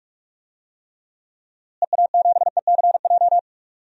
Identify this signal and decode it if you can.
Morse code, heard as a keyed tone, about 46 wpm, 700 Hz. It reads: EA7ECJ
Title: DXCC-100 Call Signs - 6 Characters - Repeated 3x - 45wpm